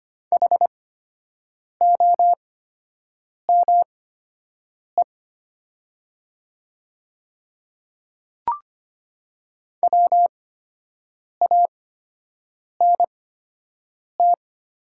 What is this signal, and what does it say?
Morse code, heard as a keyed tone, about 25 wpm, spaced as for 7 wpm, 700 Hz. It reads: HOME WANT
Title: 8x Character Spacing - Top 300 Words - 25wpm